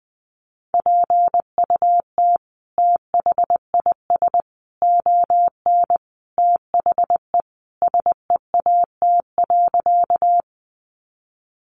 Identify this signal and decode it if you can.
Morse code, heard as a keyed tone, about 20 wpm, 700 Hz. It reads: PUT THIS ON THE SEAT.
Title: Sentences from Top 1500 Words 20wpm